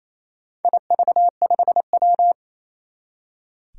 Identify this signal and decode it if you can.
Morse code, heard as a keyed tone, about 28 wpm, 700 Hz. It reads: IV5W
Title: Four Challenging Letter-Number Combinations 28wpm